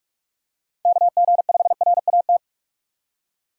Morse code, heard as a keyed tone, about 46 wpm, 700 Hz. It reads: KG5RAT
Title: USA Call Signs - 6 Characters - 3x Character Spacing - 45wpm